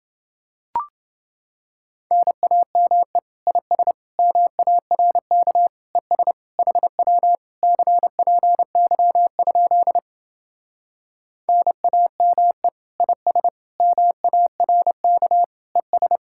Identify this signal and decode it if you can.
Morse code, heard as a keyed tone, about 29 wpm, 700 Hz. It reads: NAME IS MARK ES HW CPY? NAME IS MARK ES